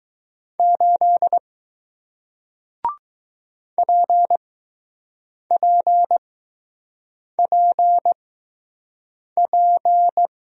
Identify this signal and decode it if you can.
Morse code, heard as a keyed tone, about 20 wpm, spaced as for 6 wpm, 700 Hz. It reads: 8 PPPP